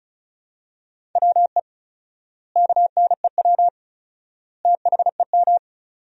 Morse code, heard as a keyed tone, about 36 wpm, 700 Hz. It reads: WE KNEW THEM
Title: 4x Word-Spacing - Sentences from Top 400 Words - 35wpm